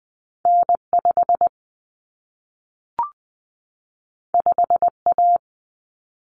Morse code, heard as a keyed tone, about 20 wpm, 700 Hz. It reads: N5 5A
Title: Common Cut Numbers - 2-Digit 20wpm